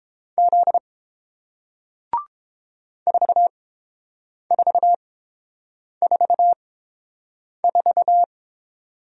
Morse code, heard as a keyed tone, about 30 wpm, spaced as for 7 wpm, 700 Hz. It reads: Z 4444